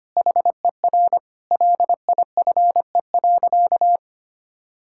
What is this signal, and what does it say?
Morse code, heard as a keyed tone, about 25 wpm, 700 Hz. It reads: HER LIFE.
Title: Sentences from Top 1000 Words - Review Part 6 - 25wpm